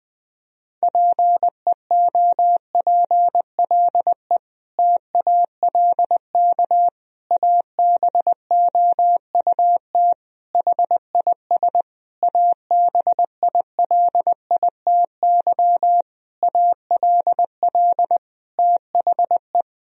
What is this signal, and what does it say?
Morse code, heard as a keyed tone, about 21 wpm, 700 Hz. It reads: PEOPLE TALK ABOUT HIS ABILITY ALL THE
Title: Sentences from Top 1000 Words - Review Part 3 - 20wpm